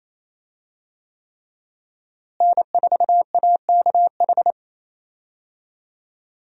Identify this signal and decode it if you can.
Morse code, heard as a keyed tone, about 28 wpm, 700 Hz. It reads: N4AKH
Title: USA Call Signs - 5 Characters - Repeated 3x - 28wpm